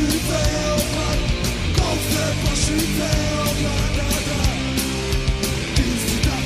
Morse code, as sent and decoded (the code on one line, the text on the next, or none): none